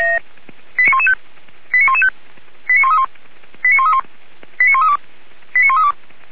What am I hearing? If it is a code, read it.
Five-tone selective call (ZVEI), heard as quick runs of five tones, tones of about 70 ms. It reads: 89115 89115 89131 89131 89132 89132
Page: /464/SDR/ZVEI/89110/